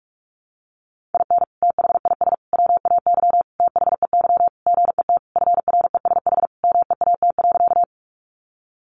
Morse code, heard as a keyed tone, about 44 wpm, 700 Hz. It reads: IN THIS WAY THEY GET FRESH MEAT.
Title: Sentences from Top 1200 Words 45wpm